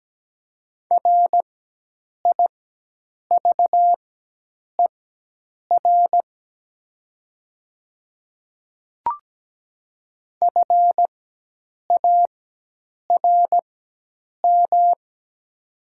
Morse code, heard as a keyed tone, about 17 wpm, spaced as for 8 wpm, 700 Hz. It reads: RIVER FARM
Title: 4x Character Spacing - Top 500 Words - 17wpm